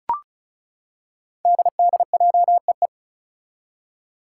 Morse code, heard as a keyed tone, about 36 wpm, 700 Hz. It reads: DDJEE
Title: Five Challenging Letter-Number Combinations 35wpm